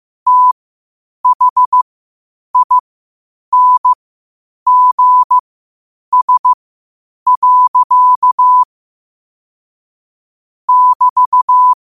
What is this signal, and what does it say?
Morse code, heard as a keyed tone, about 15 wpm, spaced as for 8 wpm, 1000 Hz. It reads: THINGS. <BT>
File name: qotdfiles / 8wpm